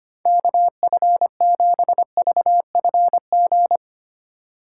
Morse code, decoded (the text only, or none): KF7VFG